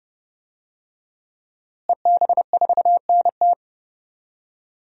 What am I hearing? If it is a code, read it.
Morse code, heard as a keyed tone, about 30 wpm, 700 Hz. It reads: EB4NT
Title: DXCC-100 Call Signs - 5 Characters - Repeated 2x - 30wpm